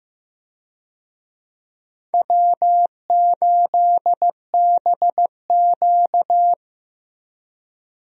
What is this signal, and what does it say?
Morse code, heard as a keyed tone, about 15 wpm, 700 Hz. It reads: W8BQ
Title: USA Call Signs - 4 Characters - 15wpm